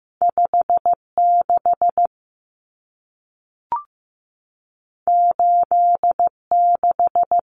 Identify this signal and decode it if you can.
Morse code, heard as a keyed tone, about 15 wpm, 700 Hz. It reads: 56 86
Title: Numbers - 2-Digit 15wpm